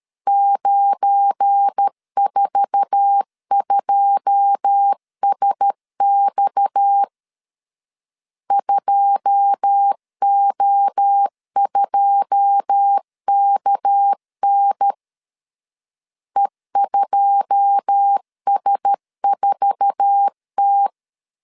Morse code, heard as a keyed tone, about 13 wpm, 800 Hz. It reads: 942SX 2O2KN E2S4T
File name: LEZIONI / 9 2 12wpm 2301